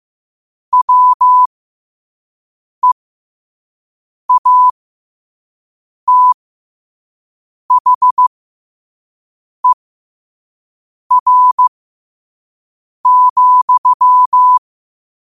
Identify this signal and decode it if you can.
Morse code, heard as a keyed tone, about 15 wpm, spaced as for 5 wpm, 1000 Hz. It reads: WEATHER,
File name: qotdfiles / 5wpm